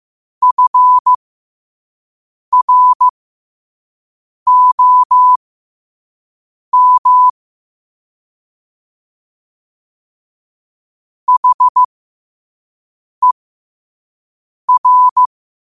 Morse code, heard as a keyed tone, about 15 wpm, spaced as for 5 wpm, 1000 Hz. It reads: FROM HER
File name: qotdfiles / 5wpm